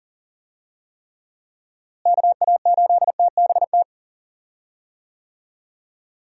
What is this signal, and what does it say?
Morse code, heard as a keyed tone, about 40 wpm, 700 Hz. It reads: KA8TBT
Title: USA Call Signs - 6 Characters - Repeated 2x - 40wpm